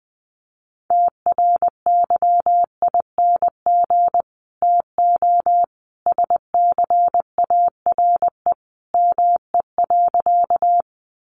Morse code, heard as a keyed tone, about 20 wpm, 700 Hz. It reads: TRYING TO SCARE ME.
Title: A Sound Of Thunder 20wpm